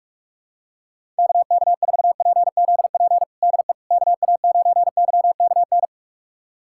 Morse code, heard as a keyed tone, about 46 wpm, 700 Hz. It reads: KK4PZP DE KA9YKN
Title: USA Callsign DE Callsign - 6 Characters - 45wpm